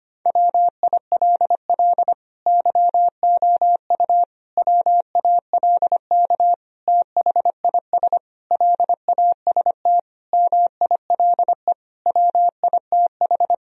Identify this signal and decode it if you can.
Morse code, heard as a keyed tone, about 25 wpm, 700 Hz. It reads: WILL YOU WALK THIS LAST MILE WITH
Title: Sentences from Top 1000 Words - Review Part 1 - 25wpm